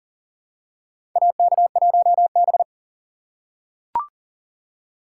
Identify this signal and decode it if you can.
Morse code, heard as a keyed tone, about 40 wpm, 700 Hz. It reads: AK1B
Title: USA Call Signs - 4 Characters - 6x Character Spacing - 40wpm